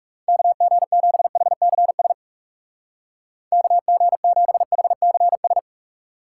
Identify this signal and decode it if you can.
Morse code, heard as a keyed tone, about 40 wpm, 700 Hz. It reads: KG7HCS KG7HCS